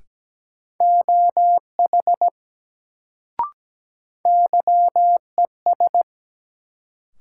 Morse code, heard as a keyed tone, about 17 wpm, 700 Hz. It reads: OH YES
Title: Sentences from Top 500 Words - New Word Drill - 17wpm